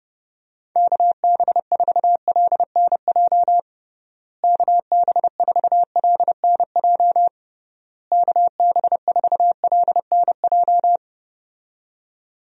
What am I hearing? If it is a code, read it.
Morse code, heard as a keyed tone, about 30 wpm, 700 Hz. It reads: KB4LNJ KB4LNJ KB4LNJ
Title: USA Call Signs - 6 Characters - Repeated 3x - 30wpm